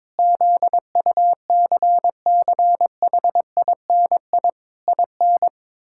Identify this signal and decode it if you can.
Morse code, heard as a keyed tone, about 22 wpm, 700 Hz. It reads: ZUCCHINI IN
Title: Pangram Challenge v2 - 22wpm